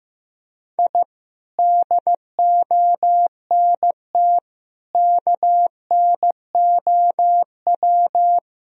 Morse code, heard as a keyed tone, about 16 wpm, 700 Hz. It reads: I DONT KNOW
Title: Sentences from Top 1000 Words - Review Part 1 - 15wpm